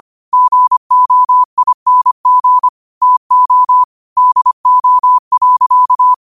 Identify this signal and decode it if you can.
Morse code, heard as a keyed tone, about 25 wpm, 1000 Hz. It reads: GOING TO DO.
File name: qotdfiles / 25wpm